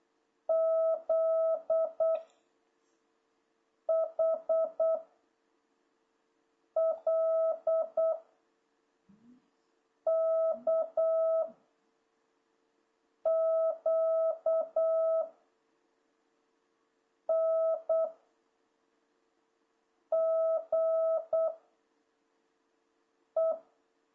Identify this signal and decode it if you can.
Morse code, heard as a keyed tone, about 8 wpm, 650 Hz. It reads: Z H L K Q N G E
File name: jh8cbh / 2026cw